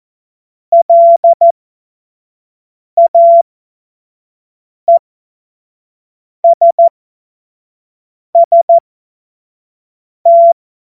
Morse code, heard as a keyed tone, about 14 wpm, spaced as for 5 wpm, 680 Hz. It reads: LAESST
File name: Morsekurs / Witze / Witz 21. Dezember 2021, 50 F70